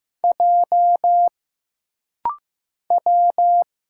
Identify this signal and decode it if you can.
Morse code, heard as a keyed tone, about 15 wpm, 700 Hz. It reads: J W
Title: Single Letter-Number Alt - Rapid-Fire 15wpm